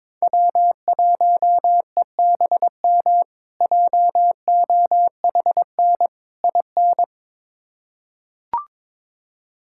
Morse code, heard as a keyed tone, about 21 wpm, 700 Hz. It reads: W1EBM JOHN IN
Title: K1USN SST QSO Practice - 22wpm